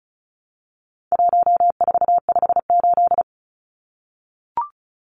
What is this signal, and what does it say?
Morse code, heard as a keyed tone, about 34 wpm, 700 Hz. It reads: 1458
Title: Numbers - 4-Digit 35wpm